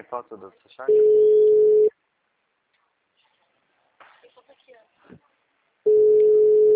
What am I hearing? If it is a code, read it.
Ringback tone (Europe, 425 Hz), with speech.
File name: out- -2002- - - .70077